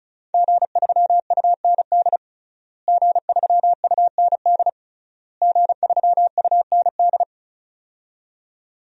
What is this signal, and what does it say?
Morse code, heard as a keyed tone, about 36 wpm, 700 Hz. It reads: G3UND G3UND G3UND
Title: DXCC-100 Call Signs - 5 Characters - Repeated 3x - 35wpm